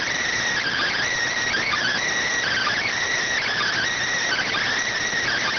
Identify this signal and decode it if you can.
Emergency Alert System (SAME) tones, heard as harsh buzzing bursts. It is heard from 0.6 to 4.6 s.